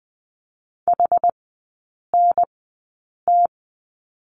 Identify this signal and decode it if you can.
Morse code, heard as a keyed tone, about 20 wpm, spaced as for 8 wpm, 690 Hz. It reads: HNT
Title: 060 Lesson 11e - Letter h - Three Characters 20wpm